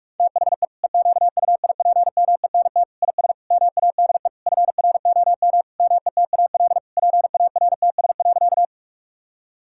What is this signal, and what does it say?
Morse code, heard as a keyed tone, about 41 wpm, 700 Hz. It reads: THE EQUIPMENT IS MADE FROM METAL PARTS.
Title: Sentences from Top 1000 Words - Review Part 6 - 45wpm